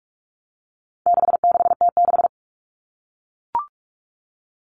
Morse code, heard as a keyed tone, about 44 wpm, 700 Hz. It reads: 66T6
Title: Common Cut Numbers - 4-Digit 45wpm